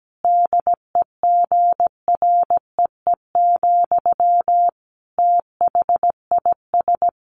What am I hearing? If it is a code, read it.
Morse code, heard as a keyed tone, about 17 wpm, 700 Hz. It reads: DEGREE, THIS
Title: Sentences from Top 1500 Words 17wpm